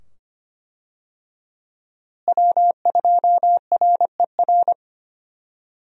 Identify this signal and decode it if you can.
Morse code, heard as a keyed tone, about 25 wpm, 700 Hz. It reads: W2RER
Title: USA Call Signs - 5 Characters - Repeated 2x - 25wpm